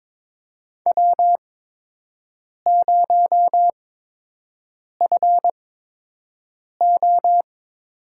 Morse code, heard as a keyed tone, about 22 wpm, spaced as for 6 wpm, 700 Hz. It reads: W0FO
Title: USA Call Signs - 5 Characters - 8x Character Spacing - 22wpm